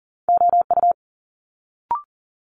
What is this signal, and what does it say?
Morse code, heard as a keyed tone, about 41 wpm, 700 Hz. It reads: OU
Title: Multi-letter Phonemes Quick Fire 40wpm